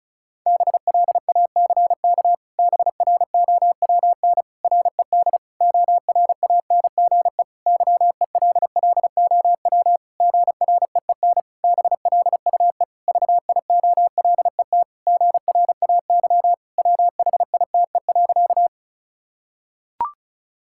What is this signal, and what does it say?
Morse code, heard as a keyed tone, about 36 wpm, 700 Hz. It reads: BLACK BROWN RED ORANGE YELLOW GREEN BLUE VIOLET GRAY WHITE.
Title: Pangram Challenge v2 - 35wpm